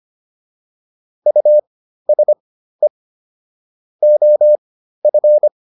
Morse code, heard as a keyed tone, about 25 wpm, spaced as for 13 wpm, 600 Hz. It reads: USE OF